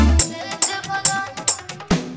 There is no code.